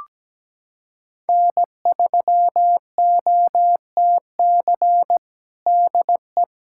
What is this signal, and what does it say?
Morse code, heard as a keyed tone, about 17 wpm, 700 Hz. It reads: N3OTC DE
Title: USA Callsign DE Callsign - 5 Characters - 17wpm